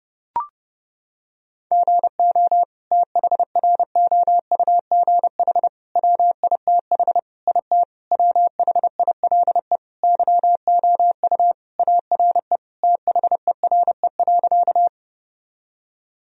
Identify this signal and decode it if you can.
Morse code, heard as a keyed tone, about 30 wpm, 700 Hz. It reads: GO THROUGH WITH IT WHILE YOU ARE THERE.
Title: Sentences from Top 1000 Words - Review Part 1 - 30wpm